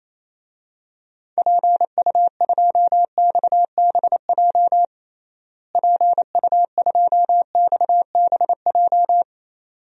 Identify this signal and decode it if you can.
Morse code, heard as a keyed tone, about 28 wpm, 700 Hz. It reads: PU2XBJ PU2XBJ
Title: DXCC-100 Call Signs - 6 Characters - Repeated 2x - 28wpm